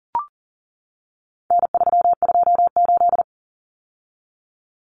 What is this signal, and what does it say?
Morse code, heard as a keyed tone, about 40 wpm, 700 Hz. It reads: N328
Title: Common Cut Numbers - 4-Digit 40wpm